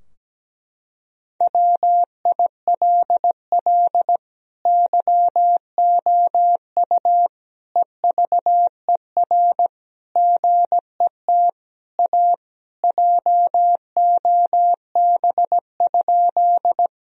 Morse code, heard as a keyed tone, about 17 wpm, 700 Hz. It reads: WILL YOU EVER GET A JOB?
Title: Sentences from Top 1000 Words - Review Part 3 - 17wpm